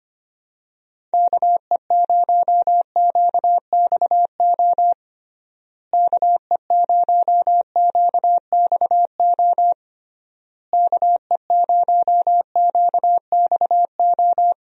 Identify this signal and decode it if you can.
Morse code, heard as a keyed tone, about 25 wpm, 700 Hz. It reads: KE0QXO KE0QXO KE0QXO
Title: USA Call Signs - 6 Characters - Repeated 3x - 25wpm